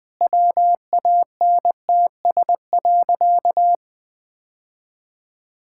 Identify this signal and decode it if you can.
Morse code, heard as a keyed tone, about 20 wpm, 700 Hz. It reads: WANTS.
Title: Sentences with 2-Word Phrases - Spoken Beginning - 20wpm